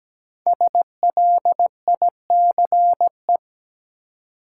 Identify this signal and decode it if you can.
Morse code, heard as a keyed tone, about 17 wpm, 700 Hz. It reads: SLICE
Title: Five Letter Words - 17wpm